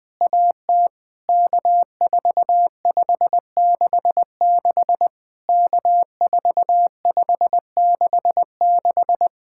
Morse code, heard as a keyed tone, about 20 wpm, 700 Hz. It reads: AT K4566 K4566
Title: POTA QSO Practice - 20wpm